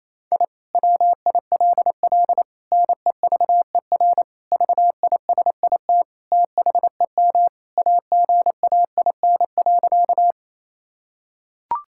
Morse code, heard as a keyed tone, about 28 wpm, 700 Hz. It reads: I WILL NEVER VISIT THEM AGAIN.
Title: Sentences from Top 1000 Words - Review Part 6 - 28wpm